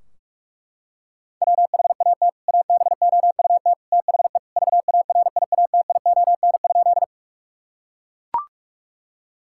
Morse code, heard as a keyed tone, about 41 wpm, 700 Hz. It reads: WHAT ABOUT THE VARIATION?
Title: Sentences from Top 1600 Words 45wpm